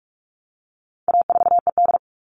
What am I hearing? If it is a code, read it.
Morse code, heard as a keyed tone, about 44 wpm, 700 Hz. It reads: A4ED
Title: All Cut Numbers - 4-Digit 45wpm